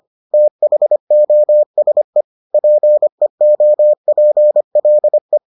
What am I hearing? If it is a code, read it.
Morse code, heard as a keyed tone, about 25 wpm, 600 Hz. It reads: THOSE PEOPLE